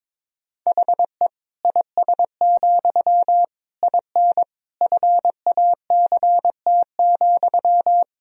Morse code, heard as a keyed tone, about 22 wpm, 700 Hz. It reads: HE IS, IN FACT,
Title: Sentences from Top 1000 Words - Review Part 3 - 22wpm